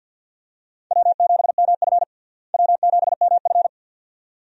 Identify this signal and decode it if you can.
Morse code, heard as a keyed tone, about 51 wpm, 700 Hz. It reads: W7MF W7MF
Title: USA Call Signs - 4 Characters - Repeated 2x - 50wpm